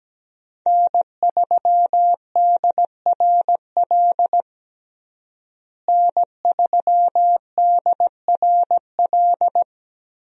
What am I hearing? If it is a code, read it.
Morse code, heard as a keyed tone, about 17 wpm, 700 Hz. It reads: N3DRL N3DRL
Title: USA Call Signs - 5 Characters - Repeated 2x - 17wpm